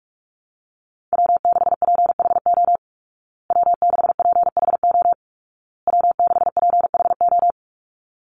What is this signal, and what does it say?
Morse code, heard as a keyed tone, about 44 wpm, 700 Hz. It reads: W6PHO W6PHO W6PHO
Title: Call Signs Repeated 2x - 45wpm